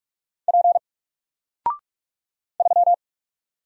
Morse code, heard as a keyed tone, about 46 wpm, 700 Hz. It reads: P 3
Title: Single Character (Letter:Number:Punctuation:Prosign) Alt 45wpm